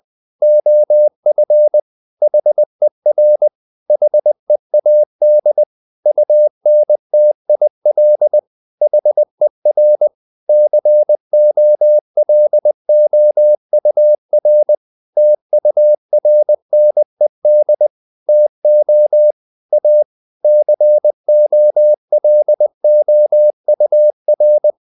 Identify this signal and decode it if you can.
Morse code, heard as a keyed tone, about 20 wpm, 600 Hz. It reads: OF HER HEAD UNTIL HER COLOUR TURNED TO A COLOUR
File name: the old man and the sea / the old man and the sea part 1 20wpm